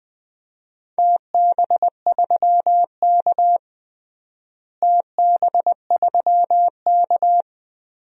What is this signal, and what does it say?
Morse code, heard as a keyed tone, about 20 wpm, 700 Hz. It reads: TB3K TB3K
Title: DXCC-100 Call Signs - 4 Characters - Repeated 2x - 20wpm